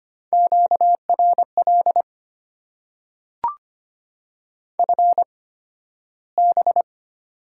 Morse code, heard as a keyed tone, about 25 wpm, spaced as for 13 wpm, 700 Hz. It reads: QRL FB
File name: Top 90 QSO Elements - 8x Character Spacing - 25wpm